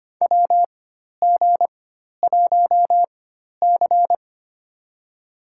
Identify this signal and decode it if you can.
Morse code, heard as a keyed tone, about 25 wpm, spaced as for 12 wpm, 700 Hz. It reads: WG1C
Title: USA Call Signs - 4 Characters - 4x Character Spacing - 25wpm